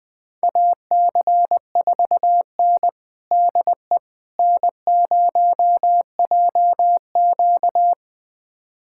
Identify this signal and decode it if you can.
Morse code, heard as a keyed tone, about 20 wpm, 700 Hz. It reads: AC4N DE N0JQ
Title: USA Callsign DE Callsign - 4 Characters - 20wpm